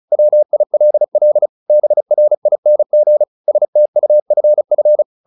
Morse code, heard as a keyed tone, about 34 wpm, 600 Hz. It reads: WILL BRING STUFF
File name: the old man and the sea 2 35wpm